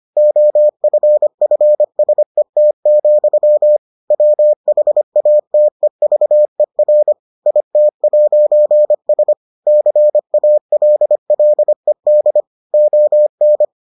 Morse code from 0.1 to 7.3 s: OFFSET, WHATEVER

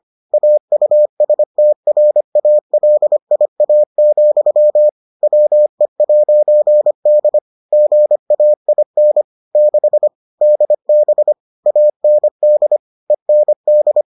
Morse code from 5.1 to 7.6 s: WE'D

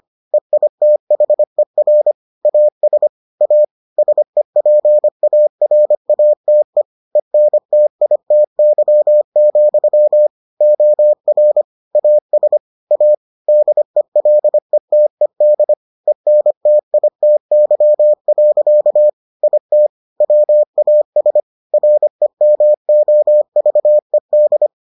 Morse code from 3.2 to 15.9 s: A SEPARATE ENTITY, OR AS A DELETED

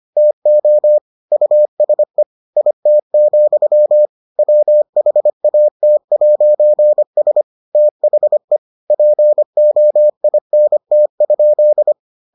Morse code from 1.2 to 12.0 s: USE IT, WHAT'S THE POINT?

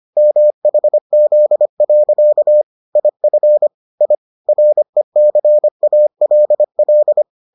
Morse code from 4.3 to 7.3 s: RECALL

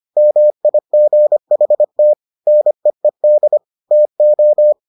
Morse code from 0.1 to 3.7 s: MIGHT NEED

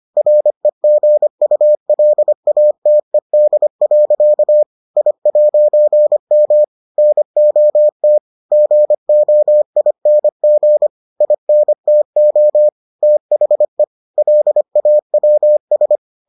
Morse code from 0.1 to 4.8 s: REGULATED.

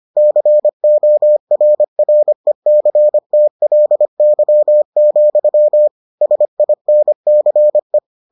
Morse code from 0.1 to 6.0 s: CORRECTLY,

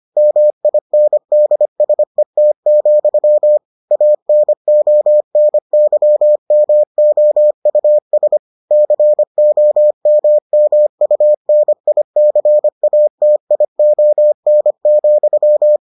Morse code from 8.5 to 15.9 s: COMMUNICATION,

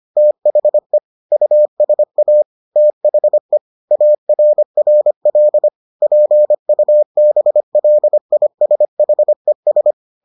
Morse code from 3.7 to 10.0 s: ARRL PUBLISHES

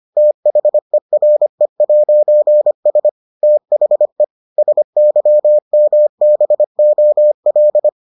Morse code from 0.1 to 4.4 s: THERE'S THE

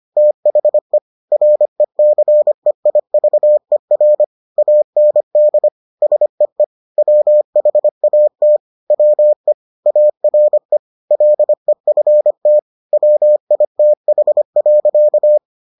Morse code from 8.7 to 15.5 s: WE ARE LEFT WITH.

